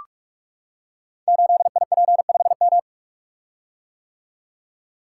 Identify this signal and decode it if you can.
Morse code, heard as a keyed tone, about 44 wpm, 690 Hz. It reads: 8IP5M
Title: Code Groups - 5 Letter:Numbers - 45wpm